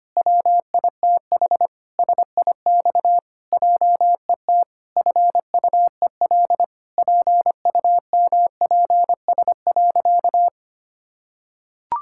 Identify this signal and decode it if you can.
Morse code, heard as a keyed tone, about 25 wpm, 700 Hz. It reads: WITH SIX JET FUEL PUMPS.
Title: Pangram Challenge v2 - 25wpm